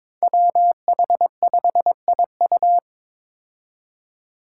Morse code, heard as a keyed tone, about 22 wpm, 700 Hz. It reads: WH5IU
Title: Five Challenging Letter-Number Combinations 22wpm